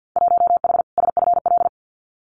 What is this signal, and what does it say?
Morse code, heard as a keyed tone, about 50 wpm, 700 Hz. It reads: 1H SFL
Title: Winter Field Day Exchanges 50wpm